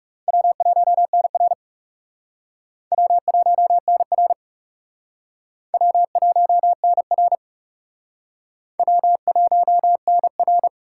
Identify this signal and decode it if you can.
Morse code, heard as a keyed tone, about 38 wpm, 700 Hz. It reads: W1NR W1NR W1NR W1NR